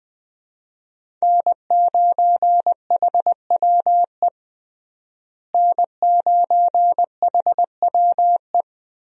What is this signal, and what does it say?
Morse code, heard as a keyed tone, about 20 wpm, 700 Hz. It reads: N9HWE N9HWE
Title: USA Call Signs - 5 Characters - Repeated 2x - 20wpm